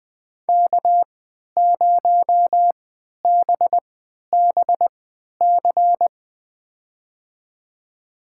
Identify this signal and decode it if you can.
Morse code, heard as a keyed tone, about 20 wpm, spaced as for 11 wpm, 700 Hz. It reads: K0BBC
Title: USA Call Signs - 3x Character Spacing - 20wpm